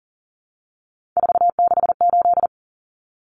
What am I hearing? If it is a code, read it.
Morse code, heard as a keyed tone, about 40 wpm, 700 Hz. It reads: 468
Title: Numbers - 3-Digit 40wpm